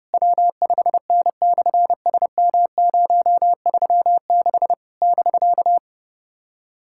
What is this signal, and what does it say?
Morse code, heard as a keyed tone, about 30 wpm, 700 Hz. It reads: W5N/SM036 <BK>
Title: SOTA QSO Practice 30wpm